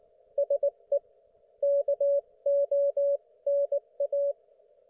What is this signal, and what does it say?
Morse code, heard as a keyed tone, about 18 wpm, 560 Hz. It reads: SE KONA